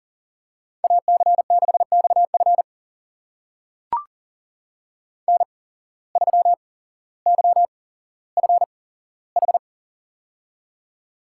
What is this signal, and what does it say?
Morse code, heard as a keyed tone, about 39 wpm, spaced as for 17 wpm, 700 Hz. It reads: AC6XF N3YFH
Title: USA Call Signs - 5 Characters - 8x Character Spacing - 40wpm